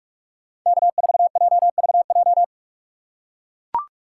Morse code, heard as a keyed tone, about 43 wpm, 700 Hz. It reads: K4JVJ
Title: USA Call Signs - 5 Characters - 6x Character Spacing - 45wpm